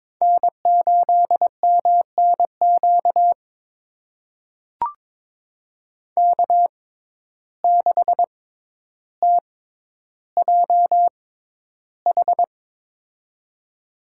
Morse code, heard as a keyed tone, about 22 wpm, spaced as for 11 wpm, 700 Hz. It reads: N8MNQ K6TJH
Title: USA Call Signs - 6x Character Spacing - 22wpm